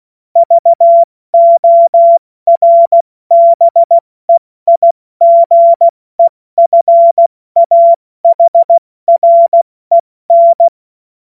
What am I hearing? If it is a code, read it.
Morse code, heard as a keyed tone, about 16 wpm, spaced as for 14 wpm, 680 Hz. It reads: VORBEIGEFAHREN
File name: Morsekurs / Witze / Witz 20. September 2022 70 F80